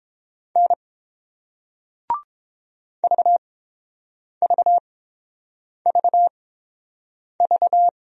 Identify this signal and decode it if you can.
Morse code, heard as a keyed tone, about 30 wpm, spaced as for 7 wpm, 700 Hz. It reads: N VVVV